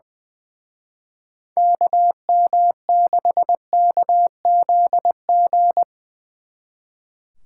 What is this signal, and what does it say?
Morse code, heard as a keyed tone, about 23 wpm, 700 Hz. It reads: KM6KZG
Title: Speed Racing - USA 2x3 Call Signs 20wpm